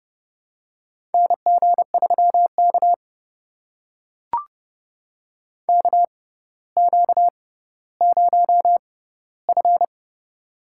Morse code, heard as a keyed tone, about 29 wpm, spaced as for 15 wpm, 700 Hz. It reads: NG3K KQ0F
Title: USA Call Signs - 4 Characters - 6x Character Spacing - 30wpm